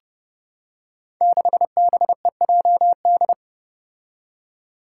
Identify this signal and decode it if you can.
Morse code, heard as a keyed tone, about 32 wpm, 700 Hz. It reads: 6BEJD
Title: Five Challenging Letter-Number Combinations 30wpm